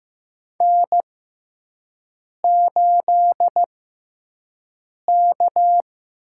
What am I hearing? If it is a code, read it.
Morse code, heard as a keyed tone, about 15 wpm, spaced as for 5 wpm, 700 Hz. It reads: N8K
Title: USA Call Signs - 6x Character Spacing - 15wpm